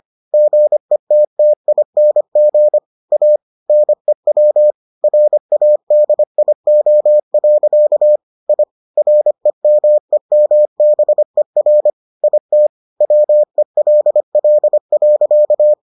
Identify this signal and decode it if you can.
Morse code, heard as a keyed tone, about 25 wpm, 600 Hz. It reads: GETTING A NEW RADIO. I REMEMBER IT WELL.